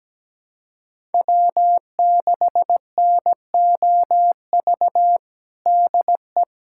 Morse code, heard as a keyed tone, about 17 wpm, 700 Hz. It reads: W6NOV DE
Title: USA Callsign DE Callsign - 5 Characters - 17wpm